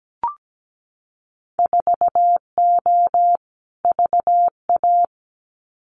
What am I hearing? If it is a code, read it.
Morse code, heard as a keyed tone, about 17 wpm, 700 Hz. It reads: 4O VA
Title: Winter Field Day Exchanges Encore 17wpm